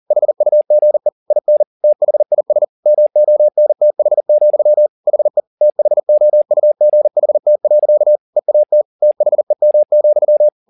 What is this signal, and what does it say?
Morse code, heard as a keyed tone, about 40 wpm, 600 Hz. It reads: HUGE IN THIS MONTH, HE THOUGHT. EAT THEM,